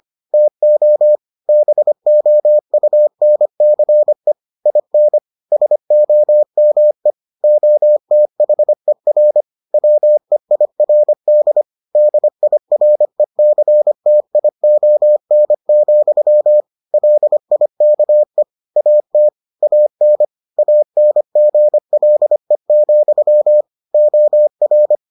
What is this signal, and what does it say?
Morse code, heard as a keyed tone, about 25 wpm, 600 Hz. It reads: TO BOUNCE IN SOME OTHER WEIRD DIRECTION, LIKE AT AN ANGLE, OR